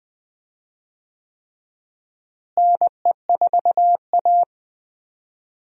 Morse code, heard as a keyed tone, about 20 wpm, 700 Hz. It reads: NE4A
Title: USA Call Signs - 4 Characters - Repeated 3x - 20wpm